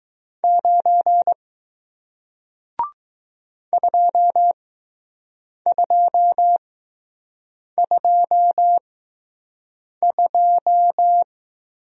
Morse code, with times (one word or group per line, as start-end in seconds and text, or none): 0.4-1.3 9
3.7-11.2 2222